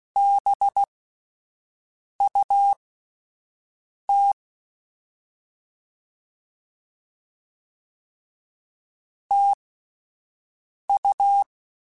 Morse code, heard as a keyed tone, about 16 wpm, spaced as for 5 wpm, 800 Hz. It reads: BUT TU
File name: QSOs Spaced